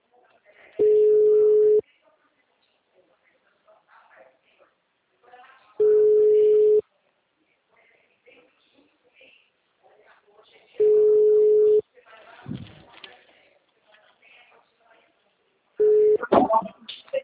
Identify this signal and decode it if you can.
Ringback tone (Europe, 425 Hz), with speech.